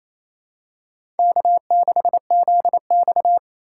Morse code, heard as a keyed tone, about 27 wpm, 700 Hz. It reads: K6ZX
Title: K1USN SST QSO Practice - 28wpm